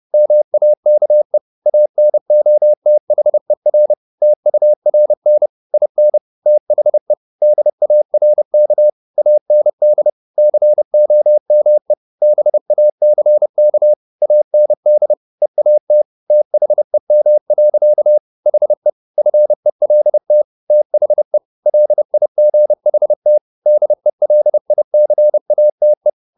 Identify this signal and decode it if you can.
Morse code, heard as a keyed tone, about 30 wpm, 600 Hz. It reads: MAKE ANOTHER TURN IN THE DARK AND COME BACK AND EAT THEM. HE FELT THE LIGHT DELICATE
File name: the old man and the sea / the old man and the sea 1 30wpm